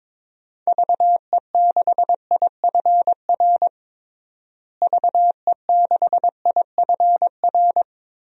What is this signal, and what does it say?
Morse code, heard as a keyed tone, about 22 wpm, 700 Hz. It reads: VE6IFR VE6IFR
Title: DXCC-100 Call Signs - 6 Characters - Repeated 3x - 22wpm